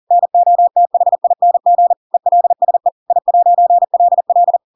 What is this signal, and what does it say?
Morse code, heard as a keyed tone, about 40 wpm, 700 Hz. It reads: NOTHING ELSE I'LL
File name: Pinocchio / Pinocchio ch7 40 40 ik1hge